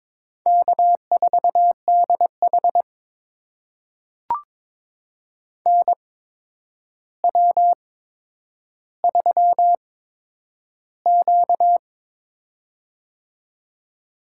Morse code, heard as a keyed tone, about 22 wpm, spaced as for 9 wpm, 700 Hz. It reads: K4DH NW3Q
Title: USA Call Signs - 4 Characters - 8x Character Spacing - 22wpm